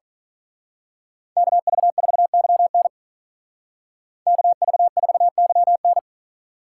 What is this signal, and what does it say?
Morse code, heard as a keyed tone, about 41 wpm, 700 Hz. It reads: KV4YN KV4YN
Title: Speed Racing - USA 2x2 Call Signs 35wpm